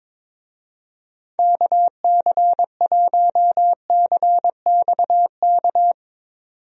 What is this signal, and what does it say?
Morse code, heard as a keyed tone, about 21 wpm, 700 Hz. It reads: KC1CXK
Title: USA Call Signs - 6 Characters - 22wpm